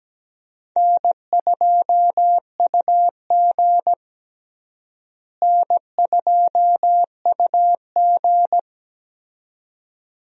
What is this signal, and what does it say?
Morse code, heard as a keyed tone, about 17 wpm, 700 Hz. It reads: N2UG N2UG
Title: USA Call Signs - 4 Characters - Repeated 2x - 17wpm